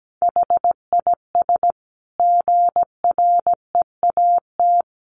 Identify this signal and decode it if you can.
Morse code, heard as a keyed tone, about 17 wpm, 700 Hz. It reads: HIS GREAT
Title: Sentences from Top 1400 Words 17wpm